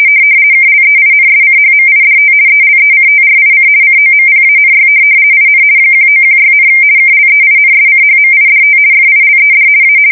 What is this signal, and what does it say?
RTTY, about 50 baud. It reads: KLEINSCHMIDT ELECTRIC COMPANY IN 1916, KLEINSCHMIDT FILED A
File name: fsk170-50B / teletype-corp